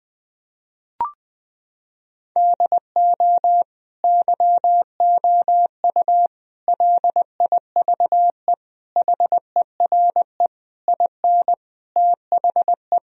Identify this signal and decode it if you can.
Morse code, heard as a keyed tone, about 20 wpm, 700 Hz. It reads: DO YOU LIVE HERE IN THE
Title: Sentences from Top 1000 Words - Review Part 1 - 20wpm